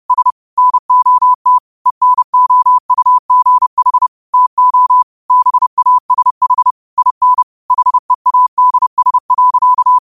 Morse code, heard as a keyed tone, about 30 wpm, 1000 Hz. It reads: S NOT ENOUGH TO BASH IN HEADS.
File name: qotdfiles / 30wpm